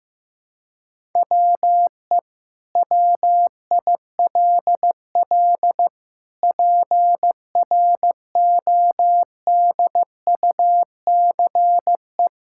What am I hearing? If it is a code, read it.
Morse code, heard as a keyed tone, about 15 wpm, 700 Hz. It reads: WE WILL PRODUCE
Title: Sentences from Top 1000 Words - Review Part 5 - 15wpm